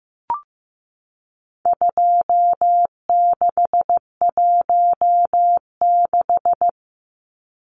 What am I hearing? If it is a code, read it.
Morse code, heard as a keyed tone, about 14 wpm, 700 Hz. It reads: 2616
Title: Numbers - 4-Digit 15wpm